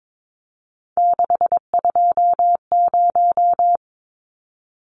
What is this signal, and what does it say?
Morse code, heard as a keyed tone, about 22 wpm, 700 Hz. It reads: 620
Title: Numbers - 3-Digit 22wpm